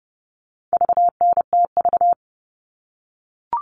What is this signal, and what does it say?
Morse code, heard as a keyed tone, about 30 wpm, 700 Hz. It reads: VNTV
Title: All Cut Numbers - 4-Digit 30wpm